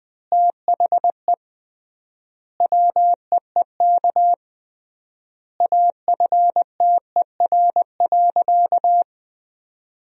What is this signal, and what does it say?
Morse code, heard as a keyed tone, about 20 wpm, 700 Hz. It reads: THE WEEK AFTER.
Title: 3x Word-Spacing - Sentences from Top 500 Words - 20wpm